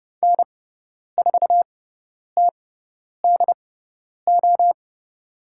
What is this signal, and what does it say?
Morse code, heard as a keyed tone, about 30 wpm, spaced as for 10 wpm, 700 Hz. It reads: N4TDO
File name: Lesson 17g - Letter u - Call Signs 30wpm